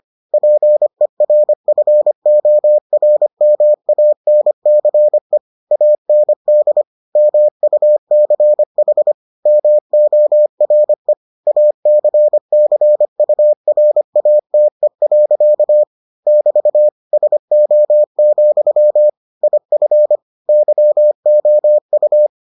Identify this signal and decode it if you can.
Morse code, heard as a keyed tone, about 25 wpm, 600 Hz. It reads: PERFORMANCE AND MUCH MORE ACCURATE. <BT> SO, IF YOU